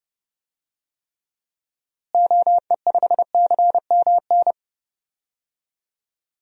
Morse code, heard as a keyed tone, about 31 wpm, 700 Hz. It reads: OE5CMN
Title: DXCC-100 Call Signs - 6 Characters - Repeated 2x - 30wpm